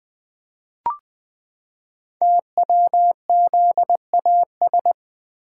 Morse code, heard as a keyed tone, about 21 wpm, 700 Hz. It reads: TWZAS
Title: Code Groups - 5 Letters - 20wpm